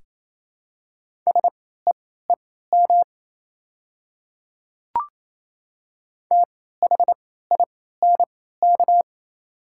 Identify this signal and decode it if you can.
Morse code, heard as a keyed tone, about 27 wpm, spaced as for 16 wpm, 690 Hz. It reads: SEEM THINK
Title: 3x Character Spacing - Top 300 Words - 28wpm